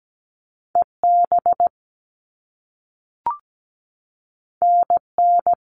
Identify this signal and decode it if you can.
Morse code, heard as a keyed tone, about 17 wpm, 700 Hz. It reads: EB NN
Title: All Cut Numbers - 2-Digit 17wpm